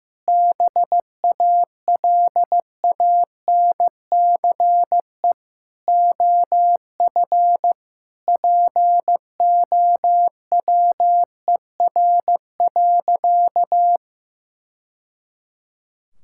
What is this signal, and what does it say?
Morse code, heard as a keyed tone, about 16 wpm, 700 Hz. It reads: BALANCE OF POWER.
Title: Sentences from Top 1000 Words - Review Part 6 - 15wpm